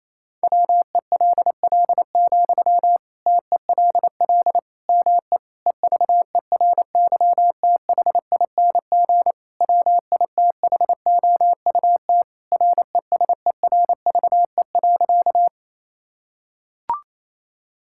Morse code, heard as a keyed tone, about 28 wpm, 700 Hz. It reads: WELL, TELL ME EVERYTHING WITHOUT RESERVE.
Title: Sentences from Top 1000 Words - Review Part 6 - 28wpm